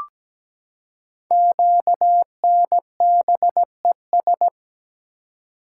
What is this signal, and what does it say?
Morse code, heard as a keyed tone, about 17 wpm, 700 Hz. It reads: QNBES
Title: Code Groups - 5 Letters - 17wpm